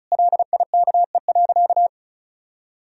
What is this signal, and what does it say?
Morse code, heard as a keyed tone, about 36 wpm, 700 Hz. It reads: LIKE.
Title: Sentences from Top 1000 Words - Review Part 2 - 35wpm